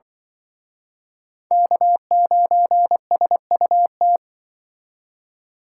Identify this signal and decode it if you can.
Morse code, heard as a keyed tone, about 25 wpm, 700 Hz. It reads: K9SUT
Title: Speed Racing - USA 1x3 Call Signs 20wpm